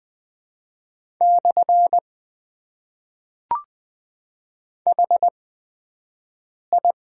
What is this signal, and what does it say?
Morse code, heard as a keyed tone, about 20 wpm, spaced as for 5 wpm, 700 Hz. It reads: / HI
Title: Top 90 QSO Elements - 8x Character Spacing - 20wpm